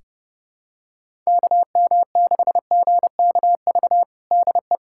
Morse code, heard as a keyed tone, about 30 wpm, 700 Hz. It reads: KM6GKV DE
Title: USA Callsign DE Callsign 30wpm